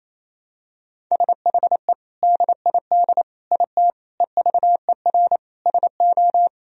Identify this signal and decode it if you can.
Morse code, heard as a keyed tone, about 28 wpm, 700 Hz. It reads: SHE DID IT EVER SO
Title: Sentences from Top 400 Words and Abbreviations 28wpm